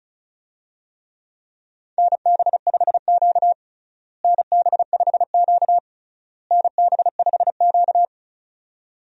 Morse code, heard as a keyed tone, about 36 wpm, 700 Hz. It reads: NB5Q NB5Q NB5Q
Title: USA Call Signs - 4 Characters - Repeated 3x - 35wpm